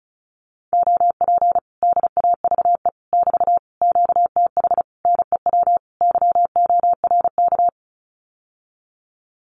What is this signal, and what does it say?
Morse code, heard as a keyed tone, about 34 wpm, 700 Hz. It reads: OP DAVE <BT> QTH NEW YORK
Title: Common QSO Phrases Encore 35wpm